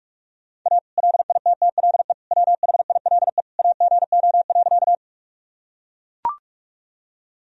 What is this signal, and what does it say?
Morse code, heard as a keyed tone, about 42 wpm, 700 Hz. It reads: A LITTLE WHILE AGO.
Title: Sentences from Top 1000 Words - Review Part 3 - 45wpm